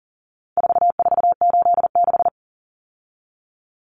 Morse code, heard as a keyed tone, about 40 wpm, 700 Hz. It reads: 4486
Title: Numbers - 4-Digit 40wpm